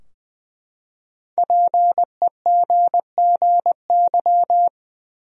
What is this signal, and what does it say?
Morse code, heard as a keyed tone, about 20 wpm, 700 Hz. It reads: PEGGY
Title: USA Common Names - Max 5 Letters - 20wpm